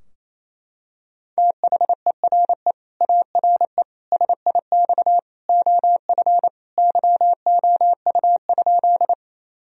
Morse code, heard as a keyed tone, about 28 wpm, 700 Hz. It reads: THERE ARE SIX OF YOU?
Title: Sentences from Top 1000 Words - Review Part 2 - 28wpm